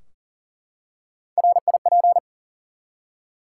Morse code, heard as a keyed tone, about 40 wpm, 700 Hz. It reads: RIP
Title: Three Letter Words - 40wpm